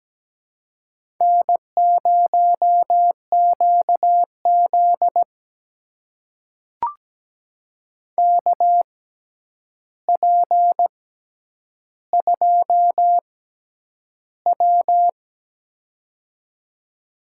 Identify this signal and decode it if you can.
Morse code, heard as a keyed tone, about 17 wpm, spaced as for 9 wpm, 700 Hz. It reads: N0QZ KP2W
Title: USA Call Signs - 4 Characters - 6x Character Spacing - 17wpm